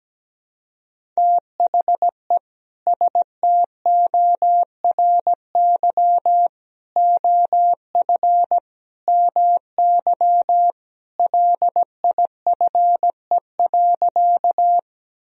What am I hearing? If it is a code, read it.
Morse code, heard as a keyed tone, about 17 wpm, 700 Hz. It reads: THE STORY OF MY LIFE.
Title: Sentences from Top 1000 Words - Review Part 1 - 17wpm